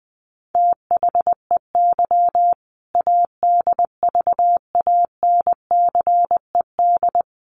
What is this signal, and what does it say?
Morse code, heard as a keyed tone, about 20 wpm, 700 Hz. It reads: THEY ADVANCED
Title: Sentences from Top 1200 Words 20wpm